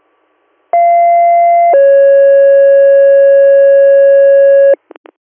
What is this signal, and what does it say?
Two-tone page, heard as a one-second tone, then a three-second tone, about 690 then 550 Hz. It is heard from 0.7 to 4.7 s.